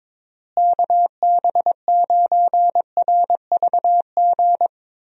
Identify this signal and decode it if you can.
Morse code, heard as a keyed tone, about 22 wpm, 700 Hz. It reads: KB9RVG
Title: USA Call Signs - 6 Characters - Repeated 2x - 22wpm